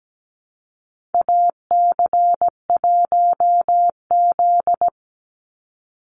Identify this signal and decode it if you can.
Morse code, heard as a keyed tone, about 17 wpm, 700 Hz. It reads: AC1Z
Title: NA SOTA Activator Callsigns 17wpm